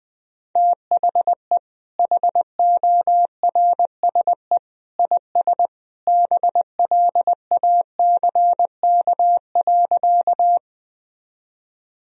Morse code, heard as a keyed tone, about 20 wpm, 700 Hz. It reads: THE HORSE IS BLACK.
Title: Sentences from Top 1000 Words - Review Part 2 - 20wpm